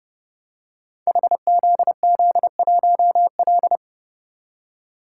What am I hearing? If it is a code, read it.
Morse code, heard as a keyed tone, about 30 wpm, 700 Hz. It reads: HZZ1L
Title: Code Groups - 5 Letter:Numbers - 30wpm